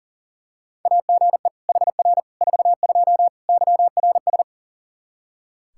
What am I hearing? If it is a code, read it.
Morse code, heard as a keyed tone, about 40 wpm, 700 Hz. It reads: AGE HR 42 YRS
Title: Practice QSO - Exchange 3 - More Info - 40wpm